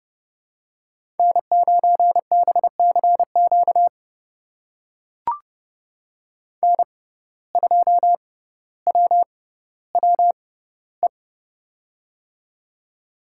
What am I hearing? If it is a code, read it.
Morse code, heard as a keyed tone, about 31 wpm, spaced as for 16 wpm, 700 Hz. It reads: N9BCQ N2WWE
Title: USA Call Signs - 5 Characters - 6x Character Spacing - 30wpm